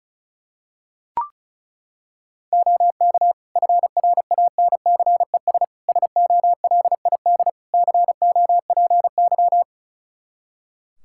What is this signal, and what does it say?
Morse code, heard as a keyed tone, about 36 wpm, 700 Hz. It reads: OK FRANCES SOLID COPY
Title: Practice QSO - Exchange 3 - More Info - 35wpm